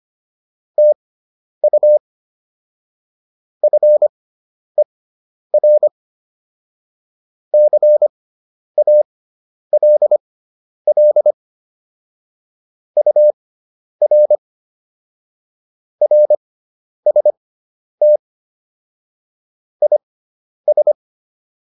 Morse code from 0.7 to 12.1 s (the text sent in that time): TU FER CALL